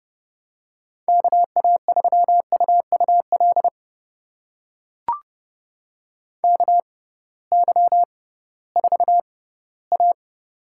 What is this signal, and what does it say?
Morse code, heard as a keyed tone, about 32 wpm, spaced as for 18 wpm, 700 Hz. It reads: KA3UUL KY4A
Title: USA Call Signs - 6x Character Spacing - 30wpm